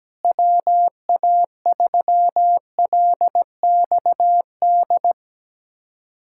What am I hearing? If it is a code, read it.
Morse code, heard as a keyed tone, about 17 wpm, 700 Hz. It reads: WA3LXD
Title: K1USN SST QSO Practice - 17wpm